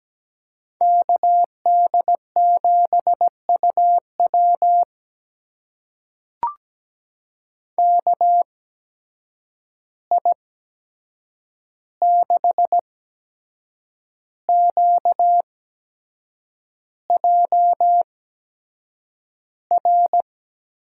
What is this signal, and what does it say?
Morse code, heard as a keyed tone, about 17 wpm, spaced as for 7 wpm, 700 Hz. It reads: KD7UW KI6QJR
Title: USA Call Signs - 8x Character Spacing - 17wpm